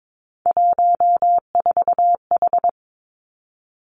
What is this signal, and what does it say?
Morse code, heard as a keyed tone, about 22 wpm, 700 Hz. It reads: 14H
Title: Three Challenging Letter-Number Combinations 22wpm